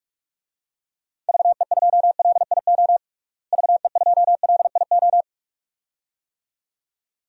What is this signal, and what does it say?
Morse code, heard as a keyed tone, about 46 wpm, 700 Hz. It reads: VE2LIO VE2LIO
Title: DXCC-100 Call Signs - 6 Characters - Repeated 2x - 45wpm